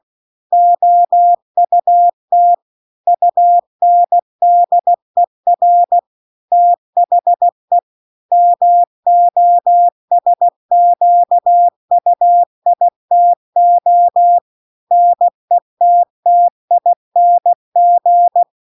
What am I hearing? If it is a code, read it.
Morse code, heard as a keyed tone, about 16 wpm, 700 Hz. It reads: OUT UNDER THE MOSQUITO NETTING
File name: 16WPM